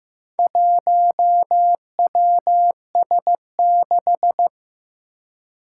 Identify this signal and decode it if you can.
Morse code, heard as a keyed tone, about 15 wpm, 700 Hz. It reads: 1WS6
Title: Four Challenging Letter-Number Combinations 15wpm